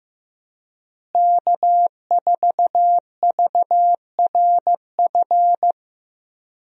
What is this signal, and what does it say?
Morse code, heard as a keyed tone, about 15 wpm, 700 Hz. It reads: K4VRF
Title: USA Call Signs - 5 Characters - Repeated 2x - 15wpm